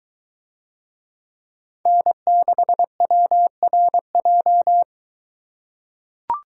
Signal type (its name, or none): Morse code